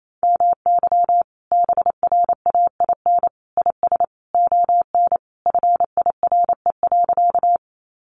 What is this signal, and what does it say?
Morse code, heard as a keyed tone, about 28 wpm, 700 Hz. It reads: MY BRAIN IS ON FIRE.
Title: Sentences from Top 1100 Words 28wpm